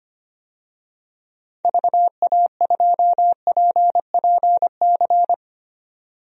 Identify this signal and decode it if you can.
Morse code, heard as a keyed tone, about 25 wpm, 700 Hz. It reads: VA2PPC
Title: DXCC-100 Call Signs - 6 Characters - 25wpm